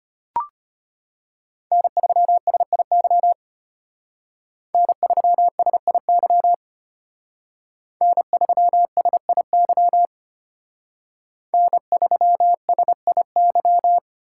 Morse code, heard as a keyed tone, about 33 wpm, 700 Hz. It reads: N3SIY N3SIY N3SIY N3SIY